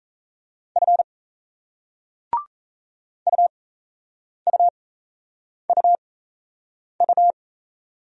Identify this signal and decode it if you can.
Morse code, heard as a keyed tone, about 36 wpm, spaced as for 8 wpm, 700 Hz. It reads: F UUUU